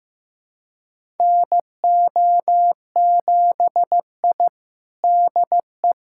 Morse code, heard as a keyed tone, about 15 wpm, 700 Hz. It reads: NO7I DE
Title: USA Callsign DE Callsign - 4 Characters - 15wpm